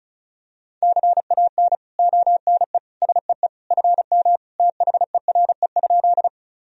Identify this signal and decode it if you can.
Morse code, heard as a keyed tone, about 34 wpm, 700 Hz. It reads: CAN ONE SEE FM THERE?
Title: Sentences from Top 100 Words and Abbreviations 35wpm